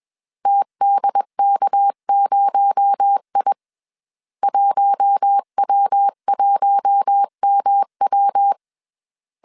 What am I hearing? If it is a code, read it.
Morse code, heard as a keyed tone, about 21 wpm, 800 Hz. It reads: TDK0I 1W1MW
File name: CW / DET / 1 6 1500C S C 20wpm 2116